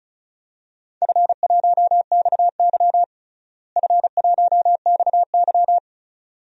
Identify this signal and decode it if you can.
Morse code, heard as a keyed tone, about 36 wpm, 700 Hz. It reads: F1XY F1XY
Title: DXCC-100 Call Signs - 4 Characters - Repeated 2x - 35wpm